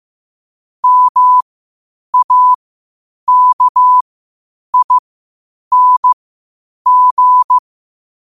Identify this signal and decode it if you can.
Morse code, heard as a keyed tone, about 15 wpm, spaced as for 8 wpm, 1000 Hz. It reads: MAKING
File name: qotdfiles / 8wpm